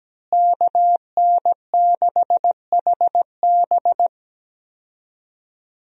Morse code, heard as a keyed tone, about 17 wpm, 700 Hz. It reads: KN6HB
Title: USA Call Signs - 5 Characters - Repeated 3x - 17wpm